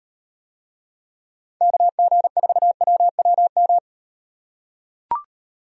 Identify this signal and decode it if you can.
Morse code, heard as a keyed tone, about 38 wpm, 700 Hz. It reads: KG4WWM